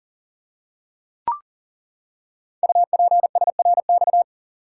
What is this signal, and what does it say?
Morse code, heard as a keyed tone, about 42 wpm, 700 Hz. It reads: UPSRX
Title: Code Groups - 5 Letters - 40wpm